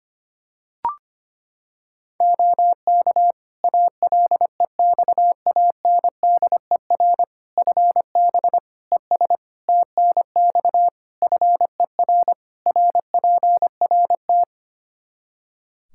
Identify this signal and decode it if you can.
Morse code, heard as a keyed tone, about 25 wpm, 700 Hz. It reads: OK ALEXANDER FB ES TNX FER RPRT
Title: K1USN SST QSO Practice 25wpm